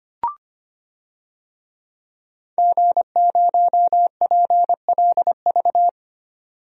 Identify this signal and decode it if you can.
Morse code, heard as a keyed tone, about 25 wpm, 700 Hz. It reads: G0PLV
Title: DXCC-100 Call Signs - 5 Characters - 25wpm